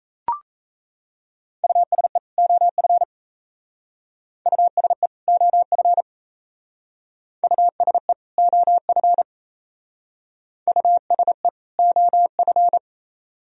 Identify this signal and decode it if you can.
Morse code, heard as a keyed tone, about 36 wpm, 700 Hz. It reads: USE OF USE OF USE OF USE OF